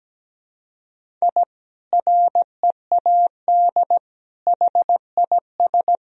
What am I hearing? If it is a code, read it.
Morse code, heard as a keyed tone, about 17 wpm, 700 Hz. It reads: I READ HIS
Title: Sentences from Top 1000 Words - Review Part 3 - 17wpm